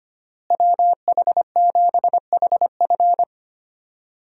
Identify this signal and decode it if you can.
Morse code, heard as a keyed tone, about 25 wpm, 700 Hz. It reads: WH7HF
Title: USA Call Signs - 5 Characters - Repeated 3x - 25wpm